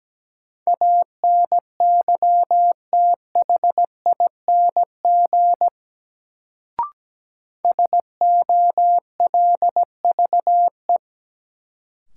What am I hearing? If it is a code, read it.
Morse code, heard as a keyed tone, about 17 wpm, 700 Hz. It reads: ANYTHING SOLVE
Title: Sentences from Top 1000 Words - New Word Drill - 17wpm